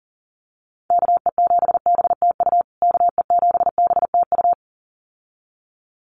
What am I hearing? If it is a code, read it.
Morse code, heard as a keyed tone, about 40 wpm, 700 Hz. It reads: KE7BTU KE7BTU
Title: USA Call Signs Repeated 2x 40wpm